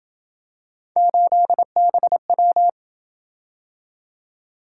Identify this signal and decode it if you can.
Morse code, heard as a keyed tone, about 28 wpm, 700 Hz. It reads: 8BW